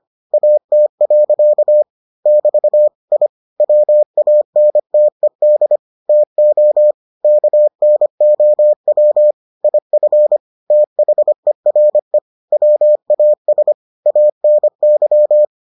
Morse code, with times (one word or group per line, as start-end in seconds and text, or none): 0.3-1.8 AT.
2.3-2.9 <BT>
3.1-3.3 I
3.6-5.8 WANTED
6.1-6.9 TO
7.2-9.3 KNOW
9.6-10.4 IF
10.7-12.2 THERE
12.5-13.7 WAS
14.1-15.4 ANY